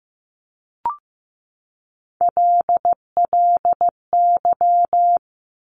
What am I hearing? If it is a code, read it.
Morse code, heard as a keyed tone, about 15 wpm, 700 Hz. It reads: LLY
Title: Top Three Letter Combinations in Words 15wpm